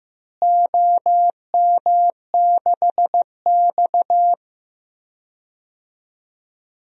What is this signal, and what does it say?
Morse code, heard as a keyed tone, about 15 wpm, 700 Hz. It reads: OM6X
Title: DXCC-100 Call Signs - 4 Characters - Repeated 3x - 15wpm